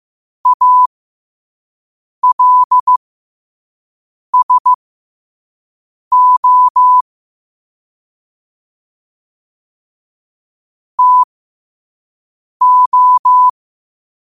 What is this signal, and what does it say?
Morse code, heard as a keyed tone, about 15 wpm, spaced as for 5 wpm, 1000 Hz. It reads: ALSO TO
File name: qotdfiles / 5wpm